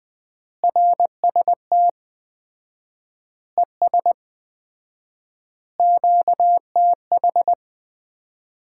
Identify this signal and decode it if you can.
Morse code, heard as a keyed tone, about 20 wpm, 700 Hz. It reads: RST ES QTH
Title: Common QSO Phrases - 4x Word-Spacing - 20wpm